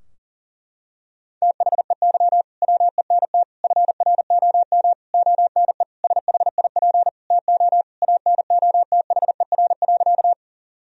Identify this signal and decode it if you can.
Morse code, heard as a keyed tone, about 40 wpm, 700 Hz. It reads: THEY WENT FROM ONE SHIP TO ANOTHER.
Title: Sentences from Top 1000 Words - Review Part 2 - 40wpm